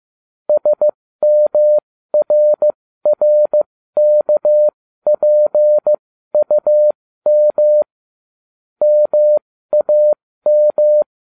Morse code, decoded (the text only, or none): SMRRKPUM MAM